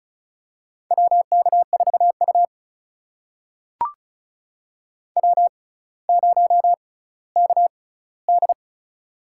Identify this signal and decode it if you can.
Morse code, heard as a keyed tone, about 34 wpm, spaced as for 18 wpm, 700 Hz. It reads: WK4U W0KD
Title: USA Call Signs - 4 Characters - 6x Character Spacing - 35wpm